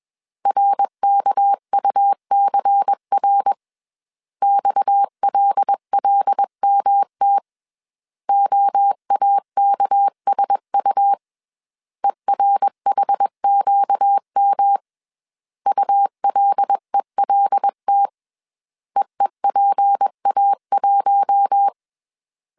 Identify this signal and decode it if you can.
Morse code, heard as a keyed tone, about 21 wpm, 800 Hz. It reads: RKUCR XLLMT OAKSU ERHQM ULELT EEPA1
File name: CW / DET / 1 6 1500C S A 20wpm 2048